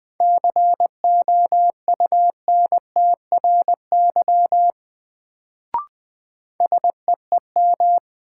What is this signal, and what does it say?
Morse code, heard as a keyed tone, about 20 wpm, 700 Hz. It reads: COUNTRY SEEM
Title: Sentences from Top 300 Words - New Word Drill - 20wpm